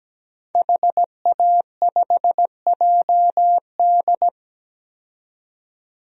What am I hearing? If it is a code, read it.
Morse code, heard as a keyed tone, about 17 wpm, 700 Hz. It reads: HA5JD
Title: DXCC-100 Call Signs - 5 Characters - 17wpm